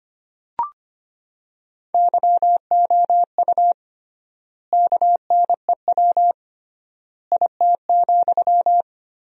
Morse code, heard as a keyed tone, about 25 wpm, 700 Hz. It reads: YOU KNEW IT,
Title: 3x Word-Spacing - Sentences from Top 500 Words - 25wpm